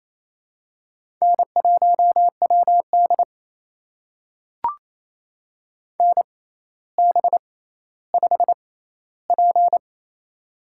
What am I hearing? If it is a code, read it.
Morse code, heard as a keyed tone, about 28 wpm, spaced as for 14 wpm, 700 Hz. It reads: N1WD NB5P
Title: USA Call Signs - 4 Characters - 6x Character Spacing - 28wpm